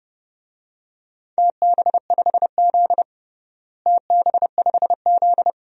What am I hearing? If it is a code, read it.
Morse code, heard as a keyed tone, about 29 wpm, 700 Hz. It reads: TB5Z TB5Z
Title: DXCC-100 Call Signs - 4 Characters - Repeated 2x - 30wpm